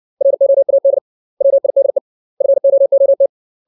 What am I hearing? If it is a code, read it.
Morse code, heard as a keyed tone, about 60 wpm, 550 Hz. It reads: ROAD WIDE FOOT
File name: 632 words / 632 Words 60WPM